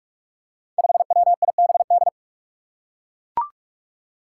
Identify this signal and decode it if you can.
Morse code, heard as a keyed tone, about 41 wpm, 700 Hz. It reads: 5WIBD
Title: Five Challenging Letter-Number Combinations 45wpm